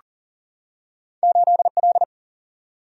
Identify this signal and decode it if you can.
Morse code, heard as a keyed tone, about 40 wpm, 700 Hz. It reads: 8L
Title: Single-Letter Single-Number 40wpm